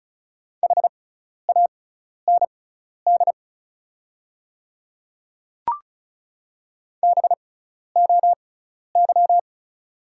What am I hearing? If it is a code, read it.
Morse code, heard as a keyed tone, about 34 wpm, spaced as for 12 wpm, 690 Hz. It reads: HAND BOY
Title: 6x Character Spacing - Top 200 Words - 35wpm